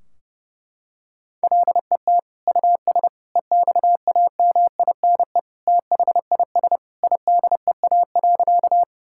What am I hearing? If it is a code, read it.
Morse code, heard as a keyed tone, about 32 wpm, 700 Hz. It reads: LET US EXAMINE THIS IDEA.
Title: Sentences from Top 1600 Words 30wpm